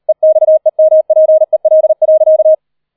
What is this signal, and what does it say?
Morse code, heard as a keyed tone, about 38 wpm, 630 Hz. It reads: EXEMPEL.